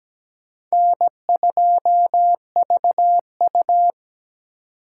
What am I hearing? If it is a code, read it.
Morse code, heard as a keyed tone, about 20 wpm, 700 Hz. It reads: N2VU
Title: Speed Racing - USA Call Signs 17wpm